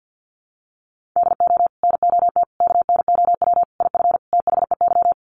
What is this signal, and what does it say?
Morse code, heard as a keyed tone, about 48 wpm, 700 Hz. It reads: DO NOT KNOW IF THEY
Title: Sets of 5 Words Encore 50wpm